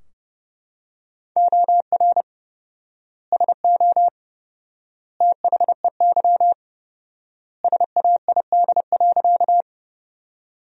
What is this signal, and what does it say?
Morse code, heard as a keyed tone, about 30 wpm, 700 Hz. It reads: OR SO THEY SAID.
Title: 4x Word-Spacing - Sentences from Top 100 Words - 30wpm